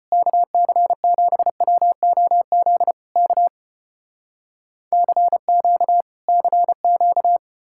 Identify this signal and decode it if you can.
Morse code, heard as a keyed tone, about 35 wpm, 700 Hz. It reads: KC7WOZ K CQ CQ